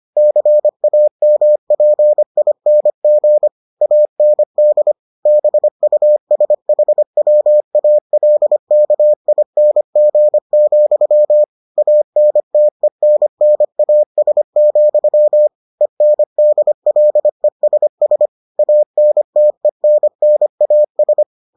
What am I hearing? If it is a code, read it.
Morse code, heard as a keyed tone, about 25 wpm, 600 Hz. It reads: CAMPING AND BUSHWALKING, ANTENNAS, ENDLESS ANTENNAS